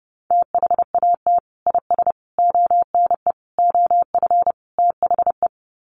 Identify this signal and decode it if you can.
Morse code, heard as a keyed tone, about 30 wpm, 700 Hz. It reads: THAT IS ONE OF THE
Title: Sets of 5 Words Encore 30wpm